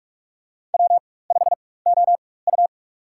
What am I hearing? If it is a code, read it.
Morse code, heard as a keyed tone, about 46 wpm, spaced as for 21 wpm, 700 Hz. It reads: W5OU
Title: USA Call Signs - 4x Character Spacing - 45wpm